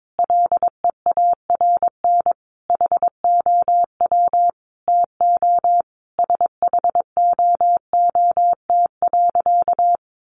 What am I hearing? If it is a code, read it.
Morse code, heard as a keyed tone, about 22 wpm, 700 Hz. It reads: LEARN HOW TO SHOOT.
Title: Sentences from Top 1200 Words 22wpm